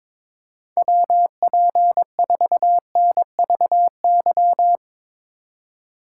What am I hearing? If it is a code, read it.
Morse code, heard as a keyed tone, about 24 wpm, 700 Hz. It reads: WP4NVY
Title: Speed Racing - USA 2x3 Call Signs 22wpm